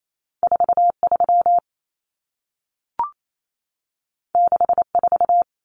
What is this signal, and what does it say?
Morse code, heard as a keyed tone, about 28 wpm, 700 Hz. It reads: 43 64
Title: Numbers - 2-Digit 28wpm